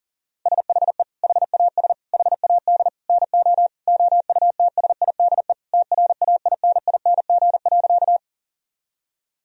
Morse code, heard as a keyed tone, about 37 wpm, 700 Hz. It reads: SHE HAS HAD NO OUTSIDE TRAINING.
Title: Sentences from Top 1000 Words - Review Part 4 - 40wpm